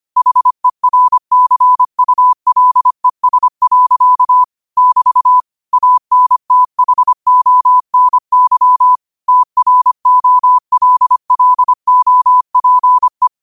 Morse code, heard as a keyed tone, about 25 wpm, 1000 Hz. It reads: HERCULES. <BT> ANTHONY TROLLOPE
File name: qotdfiles / 25wpm